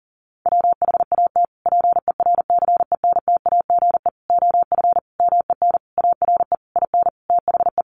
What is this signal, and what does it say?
Morse code, heard as a keyed tone, about 40 wpm, 700 Hz. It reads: WHAT PERCENTAGE OF MEN ARE IN THE
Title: Sentences from Top 1300 Words 40wpm